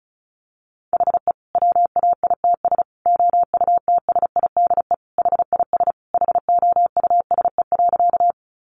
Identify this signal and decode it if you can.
Morse code, heard as a keyed tone, about 34 wpm, 700 Hz. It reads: HE WAITS OUTSIDE HIS HOUSE.
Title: Additional Easy Phrases 35wpm